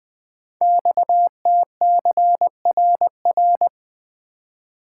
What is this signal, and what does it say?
Morse code, heard as a keyed tone, about 20 wpm, 700 Hz. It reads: XTCRR
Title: Code Groups - 5 Letters - 20wpm